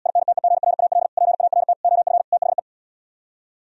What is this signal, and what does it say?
Morse code, heard as a keyed tone, about 77 wpm, 700 Hz. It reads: INTEGRAL PART OF THE